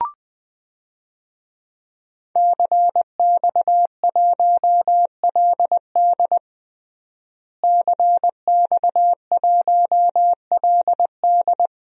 Morse code, heard as a keyed tone, about 20 wpm, 700 Hz. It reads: CX1LD CX1LD
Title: DXCC-100 Call Signs - 5 Characters - Repeated 3x - 20wpm